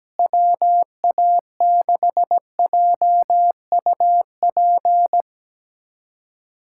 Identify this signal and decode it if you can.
Morse code, heard as a keyed tone, about 17 wpm, 700 Hz. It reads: WA6JUP
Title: USA Callsign DE Callsign - 6 Characters - 17wpm